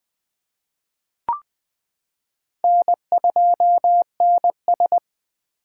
Morse code, heard as a keyed tone, about 19 wpm, 700 Hz. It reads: N2NS
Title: K1USN SST QSO Practice - 20wpm